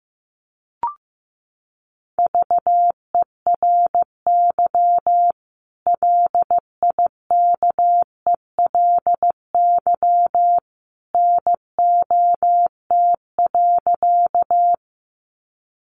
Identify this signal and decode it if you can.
Morse code, heard as a keyed tone, about 15 wpm, 700 Hz. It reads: VERY LIKELY NOT.
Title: Sentences from Top 600 Words 15wpm